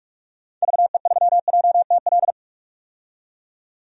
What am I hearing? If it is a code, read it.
Morse code, heard as a keyed tone, about 43 wpm, 700 Hz. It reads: VE3JTL
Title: DXCC-100 Call Signs - 6 Characters - 45wpm